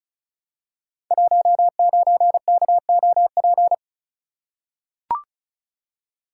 Morse code, heard as a keyed tone, about 36 wpm, 700 Hz. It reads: 19KOP
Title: Code Groups - 5 Letter:Numbers - 35wpm